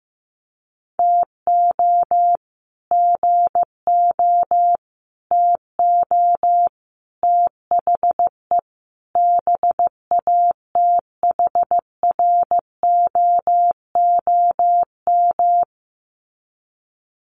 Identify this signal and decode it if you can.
Morse code, heard as a keyed tone, about 15 wpm, 700 Hz. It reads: TO GO TO THE BATHROOM
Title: Sets of 5 Words Encore 15wpm